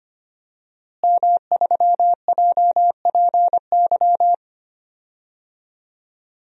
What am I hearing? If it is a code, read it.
Morse code, heard as a keyed tone, about 25 wpm, 700 Hz. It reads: M3JPY
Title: DXCC-100 Call Signs - 5 Characters - Repeated 3x - 25wpm